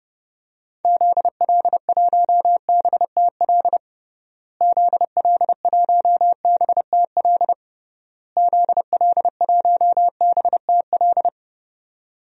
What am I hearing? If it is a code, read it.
Morse code, heard as a keyed tone, about 30 wpm, 700 Hz. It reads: ZL1BTL ZL1BTL ZL1BTL
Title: DXCC-100 Call Signs - 6 Characters - Repeated 3x - 30wpm